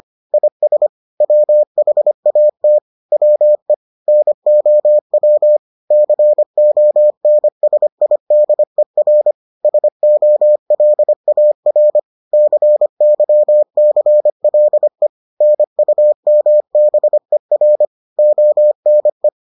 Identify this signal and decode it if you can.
Morse code, heard as a keyed tone, about 25 wpm, 600 Hz. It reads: IS WHAT WE NOW CONSIDER SOLAR CYCLE NUMBER ONE